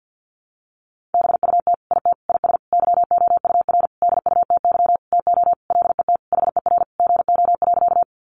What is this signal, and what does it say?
Morse code, heard as a keyed tone, about 50 wpm, 700 Hz. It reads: BUT IT IS YOUR DUTY TO LET HER GO.
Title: Sentences from Top 1500 Words 50wpm